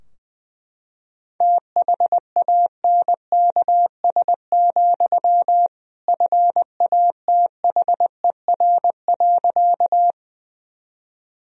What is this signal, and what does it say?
Morse code, heard as a keyed tone, about 20 wpm, 700 Hz. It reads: THANKS, FATHER.
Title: Sentences from Top 1000 Words - Review Part 3 - 20wpm